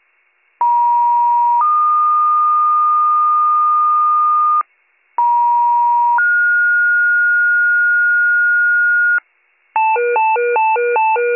Two-tone page, heard as a one-second tone, then a three-second tone, about 950 then 1220 Hz, about 950 then 1430 Hz, with noise. It is heard from 0.6 to 9.2 s.